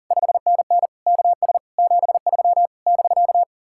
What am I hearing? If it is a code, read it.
Morse code, heard as a keyed tone, about 49 wpm, 700 Hz. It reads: 5NN KS 73 <BK>
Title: Speed-Racing Common QSO Phrases - 40wpm